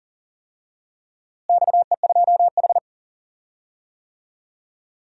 Morse code, heard as a keyed tone, about 40 wpm, 700 Hz. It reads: XE2H
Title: DXCC-100 Call Signs - 4 Characters - Repeated 2x - 40wpm